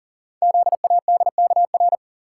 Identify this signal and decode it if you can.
Morse code, heard as a keyed tone, about 41 wpm, 690 Hz. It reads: ZADKR
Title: Code Groups - 5 Letters - 40wpm